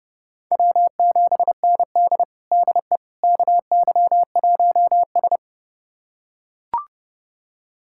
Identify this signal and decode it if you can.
Morse code, heard as a keyed tone, about 30 wpm, 700 Hz. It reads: W7ND DE KY1S
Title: USA Callsign DE Callsign - 4 Characters - 30wpm